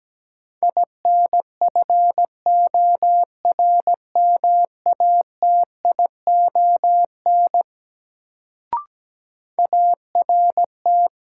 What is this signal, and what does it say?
Morse code, heard as a keyed tone, about 17 wpm, 700 Hz. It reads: INFORMATION ART
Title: Sentences from Top 600 Words - New Word Drill - 17wpm